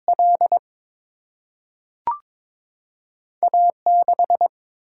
Morse code, heard as a keyed tone, about 22 wpm, 700 Hz. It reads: L A 6
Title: Single-Letter Single-Number 22wpm